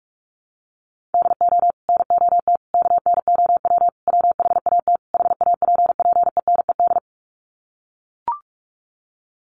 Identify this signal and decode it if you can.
Morse code, heard as a keyed tone, about 44 wpm, 700 Hz. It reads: DO NOT KNOW WHAT HAPPENED
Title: Sets of 5 Words Encore2 45wpm